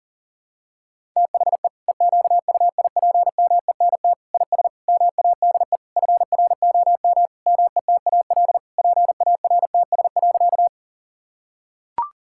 Morse code, heard as a keyed tone, about 37 wpm, 700 Hz. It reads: THE EQUIPMENT IS MADE FROM METAL PARTS.
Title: Sentences from Top 1000 Words - Review Part 6 - 40wpm